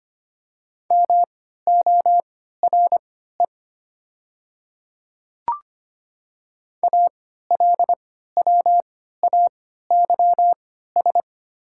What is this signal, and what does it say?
Morse code, heard as a keyed tone, about 25 wpm, spaced as for 14 wpm, 700 Hz. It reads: MORE ALWAYS
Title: 3x Character Spacing - Top 500 Words - 25wpm